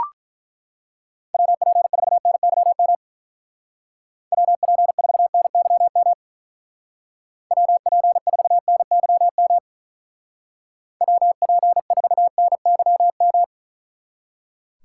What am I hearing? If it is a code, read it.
Morse code, heard as a keyed tone, about 43 wpm, 700 Hz. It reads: WP4NYM WP4NYM WP4NYM WP4NYM
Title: Speed Racing - USA Call Signs 35wpm